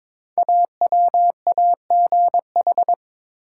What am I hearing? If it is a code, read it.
Morse code, heard as a keyed tone, about 22 wpm, 700 Hz. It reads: AWAGH
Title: Code Groups - 5 Letter:Numbers - 22wpm